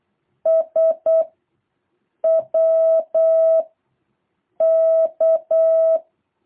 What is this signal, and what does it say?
Morse code, heard as a keyed tone, about 8 wpm, 660 Hz. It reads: S W K
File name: jh8cbh / 2023cw